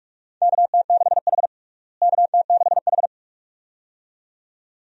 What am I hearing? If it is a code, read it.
Morse code, heard as a keyed tone, about 46 wpm, 700 Hz. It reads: KT6H KT6H
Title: USA Call Signs - 4 Characters - Repeated 2x - 45wpm